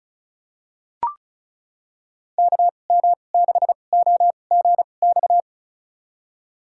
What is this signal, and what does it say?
Morse code, heard as a keyed tone, about 36 wpm, spaced as for 25 wpm, 700 Hz. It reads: KM6OGX
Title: USA Call Signs - 6 Characters - 2x Character Spacing - 35wpm